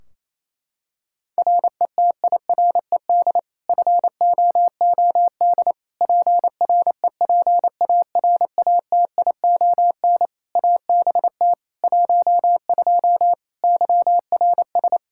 Morse code, heard as a keyed tone, about 28 wpm, 700 Hz. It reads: RETIRED FOOD PREPARATION ABT 12 YRS
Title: K1USN SST QSO Practice 28wpm